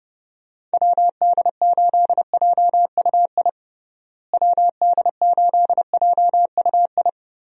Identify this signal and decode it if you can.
Morse code, heard as a keyed tone, about 30 wpm, 700 Hz. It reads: WD8JUI WD8JUI
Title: USA Call Signs - 6 Characters - Repeated 3x - 30wpm